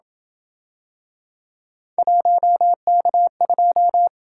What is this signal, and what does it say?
Morse code, heard as a keyed tone, about 27 wpm, 700 Hz. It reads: 1K2